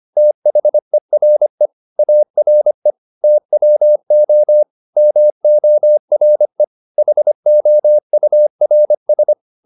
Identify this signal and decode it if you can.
Morse code, heard as a keyed tone, about 25 wpm, 600 Hz. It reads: THERE ARE TWO MORE HOURS
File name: the old man and the sea part 1 25wpm